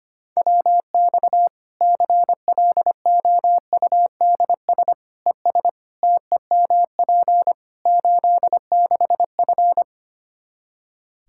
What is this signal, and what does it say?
Morse code, heard as a keyed tone, about 25 wpm, 700 Hz. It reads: WX CLOUDS ES TEMP 86F
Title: Practice QSO - Exchange 2 - Info - 25wpm